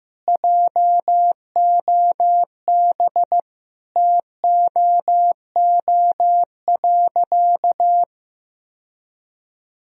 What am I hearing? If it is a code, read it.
Morse code, heard as a keyed tone, about 15 wpm, 700 Hz. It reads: JOB TOO.
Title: Sentences from Top 1000 Words - Review Part 3 - 15wpm